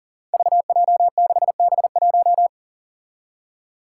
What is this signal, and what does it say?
Morse code, heard as a keyed tone, about 40 wpm, 700 Hz. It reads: VJ6B1
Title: Five Challenging Letter-Number Combinations 40wpm